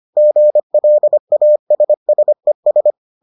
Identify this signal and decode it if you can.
Morse code, heard as a keyed tone, about 25 wpm, 600 Hz. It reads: GLASSES